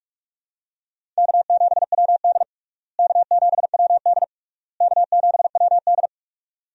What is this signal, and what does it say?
Morse code, heard as a keyed tone, about 46 wpm, 700 Hz. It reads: K7WD K7WD K7WD
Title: USA Call Signs - 4 Characters - Repeated 3x - 45wpm